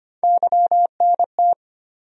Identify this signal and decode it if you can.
Morse code, heard as a keyed tone, about 25 wpm, 700 Hz. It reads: YNT
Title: USA Call Sign Postfixes - 3 Letters 25wpm